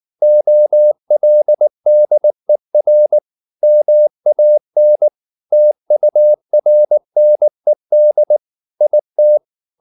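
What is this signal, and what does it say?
Morse code, heard as a keyed tone, about 19 wpm, 600 Hz. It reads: OLDER MAN TURNED IT
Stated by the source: Morse code at 19 wpm from CW med Gnister, reading Ken Follett's novel Never in English.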